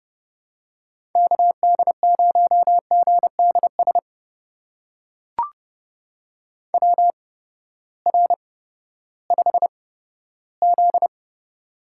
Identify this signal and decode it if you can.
Morse code, heard as a keyed tone, about 30 wpm, spaced as for 15 wpm, 700 Hz. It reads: KD0GDS WR5Z
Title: USA Call Signs - 8x Character Spacing - 30wpm